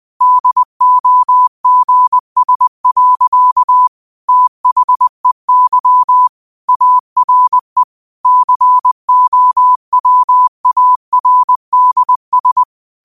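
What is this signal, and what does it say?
Morse code, heard as a keyed tone, about 20 wpm, 1000 Hz. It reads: DOGS. THEY ARE COWARDS
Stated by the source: 20 WPM Morse code quotes, QOTD at 20 WPM.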